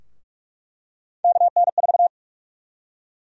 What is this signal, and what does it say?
Morse code, heard as a keyed tone, about 41 wpm, 700 Hz. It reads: KN4
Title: USA Call Sign Prefixes - 45wpm